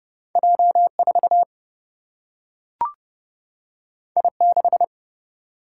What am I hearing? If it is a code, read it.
Morse code, heard as a keyed tone, about 30 wpm, 700 Hz. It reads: J4 I6
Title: Single-Letter Single-Number 30wpm